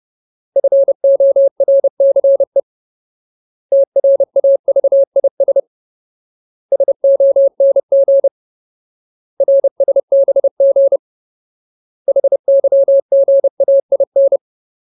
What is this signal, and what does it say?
Morse code, heard as a keyed tone, about 30 wpm, 550 Hz. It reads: FORCE TRAVIS SONG RSBG HYGAIN
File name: Ultimate Word List / 30WPM Ultimate